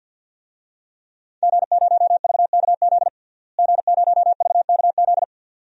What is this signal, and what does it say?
Morse code, heard as a keyed tone, about 50 wpm, 700 Hz. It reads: G0VKZ G0VKZ
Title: DXCC-100 Call Signs - 5 Characters - Repeated 2x - 50wpm